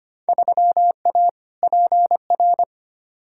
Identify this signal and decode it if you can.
Morse code, heard as a keyed tone, about 25 wpm, 700 Hz. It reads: 3A PR
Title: Field Day Exchanges Encore - 25wpm